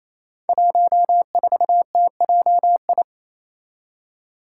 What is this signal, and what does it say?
Morse code, heard as a keyed tone, about 28 wpm, 700 Hz. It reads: 14TJI
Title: Five Challenging Letter-Number Combinations 28wpm